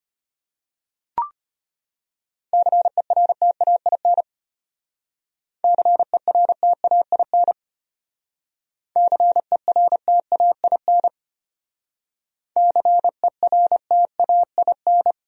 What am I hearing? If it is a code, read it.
Morse code, heard as a keyed tone, about 32 wpm, 700 Hz. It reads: CERTAIN CERTAIN CERTAIN CERTAIN